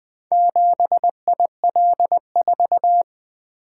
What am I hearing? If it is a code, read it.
Morse code, heard as a keyed tone, about 20 wpm, 700 Hz. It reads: 7IL4
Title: Four Letter-Number Combinations - 20wpm